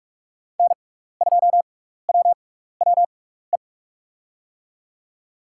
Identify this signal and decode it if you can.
Morse code, heard as a keyed tone, about 47 wpm, spaced as for 16 wpm, 700 Hz. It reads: N2WWE
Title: USA Call Signs - 6x Character Spacing - 45wpm